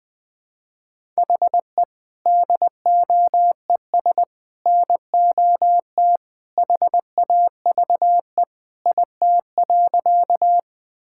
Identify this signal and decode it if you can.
Morse code, heard as a keyed tone, about 20 wpm, 700 Hz. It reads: HE DOES NOT HAVE IT.
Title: Sentences with 3-Word Phrases - 20wpm